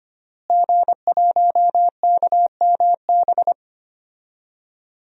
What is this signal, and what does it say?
Morse code, heard as a keyed tone, about 25 wpm, 700 Hz. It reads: G1KMB
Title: DXCC-100 Call Signs - 5 Characters - Repeated 2x - 25wpm